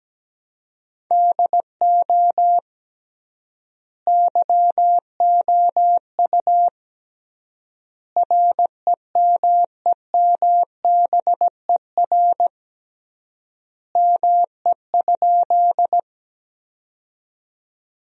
Morse code, heard as a keyed tone, about 17 wpm, 700 Hz. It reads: DO YOU REMEMBER ME?
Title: 3x Word-Spacing - Sentences from Top 400 Words - 17wpm